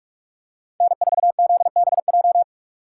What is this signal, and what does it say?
Morse code, heard as a keyed tone, about 45 wpm, 700 Hz. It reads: N4ZBJ